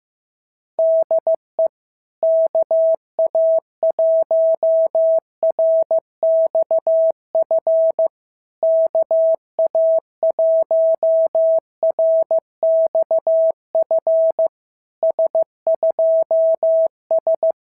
Morse code, heard as a keyed tone, about 14 wpm, 650 Hz. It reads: DE KA1RXF KA1RXF S2S
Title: SOTA QSO Practice 15wpm